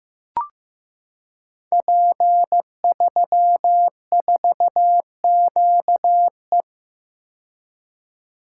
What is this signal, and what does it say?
Morse code, heard as a keyed tone, about 15 wpm, 700 Hz. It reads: P34QE
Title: Code Groups - 5 Letter:Numbers - 15wpm